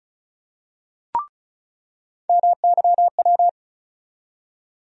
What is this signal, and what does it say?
Morse code, heard as a keyed tone, about 35 wpm, 700 Hz. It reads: MYW
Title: USA Call Sign Postfixes - 3 Letters 35wpm